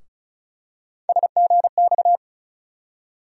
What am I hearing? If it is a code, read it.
Morse code, heard as a keyed tone, about 36 wpm, 700 Hz. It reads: SGX
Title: USA Call Sign Postfixes - 3 Letters 35wpm